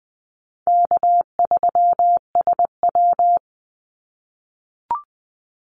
Morse code, heard as a keyed tone, about 19 wpm, 700 Hz. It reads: K3SW
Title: Call Signs - 20wpm